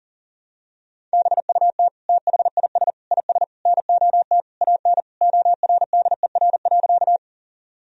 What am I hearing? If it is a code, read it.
Morse code, heard as a keyed tone, about 40 wpm, 700 Hz. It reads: BUT THIS IS NOT AN ORDER.
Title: Sentences from Top 1000 Words - Review Part 2 - 40wpm